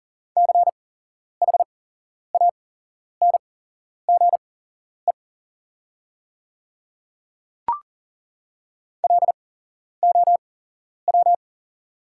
Morse code, heard as a keyed tone, about 40 wpm, spaced as for 11 wpm, 690 Hz. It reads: CHANGE LOW
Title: 8x Character Spacing - Top 200 Words - 40wpm